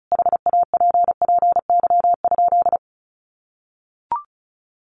Morse code, heard as a keyed tone, about 34 wpm, 700 Hz. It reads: HAPPY?
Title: Sentences from Top 1100 Words 35wpm